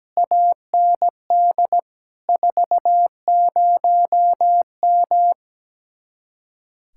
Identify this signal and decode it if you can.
Morse code, heard as a keyed tone, about 17 wpm, 700 Hz. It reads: AND 40M
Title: Speed-Racing Common QSO Phrases - 17wpm